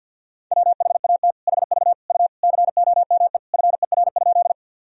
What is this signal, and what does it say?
Morse code, heard as a keyed tone, about 50 wpm, 700 Hz. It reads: WHAT HV U COME FER?
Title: Sentences from Top 100 Words and Abbreviations 50wpm